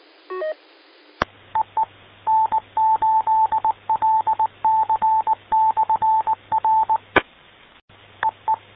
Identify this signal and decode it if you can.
Morse code, heard as a keyed tone, about 20 wpm, 880 Hz. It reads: EE N8LC/R EE